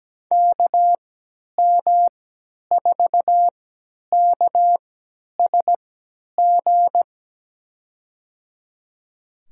Morse code, heard as a keyed tone, about 17 wpm, spaced as for 10 wpm, 700 Hz. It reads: KM4KSG
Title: USA Call Signs - 3x Character Spacing - 17wpm